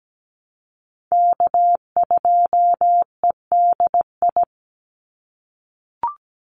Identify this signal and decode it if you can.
Morse code, heard as a keyed tone, about 17 wpm, 700 Hz. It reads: K2EDI
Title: USA Call Signs 17wpm